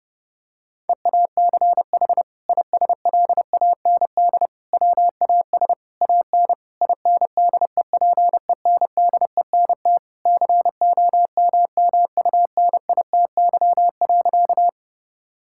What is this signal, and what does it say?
Morse code, heard as a keyed tone, about 30 wpm, 700 Hz. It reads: EACH ISLAND WAS AN INDEPENDENT COMMUNITY.
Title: Sentences from Top 1000 Words - Review Part 6 - 30wpm